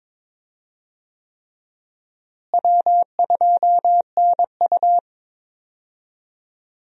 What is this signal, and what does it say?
Morse code, heard as a keyed tone, about 22 wpm, 700 Hz. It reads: W2NU
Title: USA Call Signs - 4 Characters - Repeated 3x - 22wpm